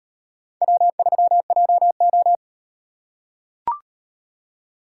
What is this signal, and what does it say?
Morse code, heard as a keyed tone, about 38 wpm, 700 Hz. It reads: W3JO